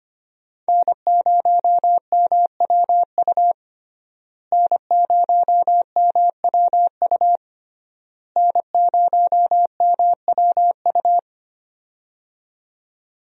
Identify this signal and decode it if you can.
Morse code, heard as a keyed tone, about 25 wpm, 700 Hz. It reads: N0MWU N0MWU N0MWU
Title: USA Call Signs - 5 Characters - Repeated 3x - 25wpm